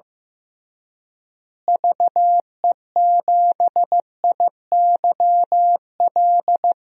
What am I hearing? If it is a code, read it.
Morse code, heard as a keyed tone, about 15 wpm, 700 Hz. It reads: VE7IYL
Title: DXCC-100 Call Signs - 6 Characters - Repeated 3x - 15wpm